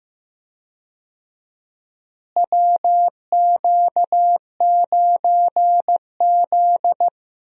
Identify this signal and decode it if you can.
Morse code, heard as a keyed tone, about 15 wpm, 700 Hz. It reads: WQ9Z
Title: USA Call Signs - 4 Characters - Repeated 3x - 15wpm